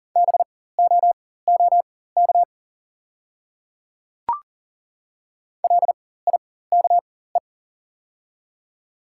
Morse code, heard as a keyed tone, about 40 wpm, spaced as for 19 wpm, 700 Hz. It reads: BOOK LIKE
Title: 4x Character Spacing - Top 300 Words - 40wpm